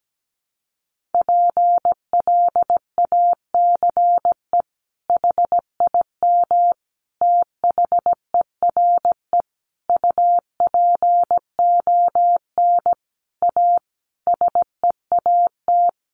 Morse code, heard as a keyed tone, about 17 wpm, 700 Hz. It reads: PLACE HIM THERE UPON A SEAT
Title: Sentences from Top 1500 Words 17wpm